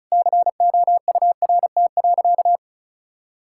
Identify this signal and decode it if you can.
Morse code, heard as a keyed tone, about 36 wpm, 700 Hz. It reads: COURT.
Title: Sentences from Top 1600 Words 35wpm